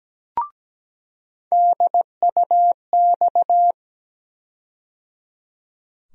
Morse code, heard as a keyed tone, about 17 wpm, 700 Hz. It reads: DUX
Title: USA Call Sign Postfixes - 3 Letters 17wpm